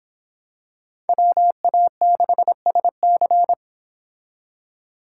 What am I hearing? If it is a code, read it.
Morse code, heard as a keyed tone, about 26 wpm, 700 Hz. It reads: WA6SC